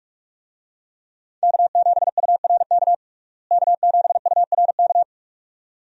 Morse code, heard as a keyed tone, about 46 wpm, 700 Hz. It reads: K7URK K7URK
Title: USA Call Signs - 5 Characters - Repeated 2x - 45wpm